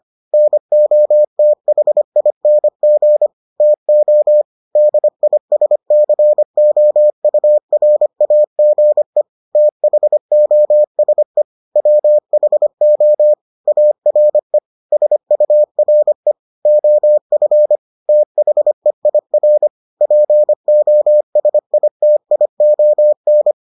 Morse code, heard as a keyed tone, about 25 wpm, 600 Hz. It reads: NOTHING TO DISCOURAGE THOSE WHO ARE SURE OF THEIR POSITION